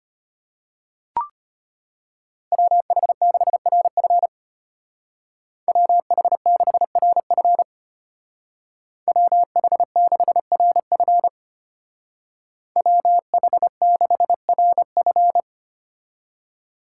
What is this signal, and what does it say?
Morse code, heard as a keyed tone, about 32 wpm, 700 Hz. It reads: WH6RF WH6RF WH6RF WH6RF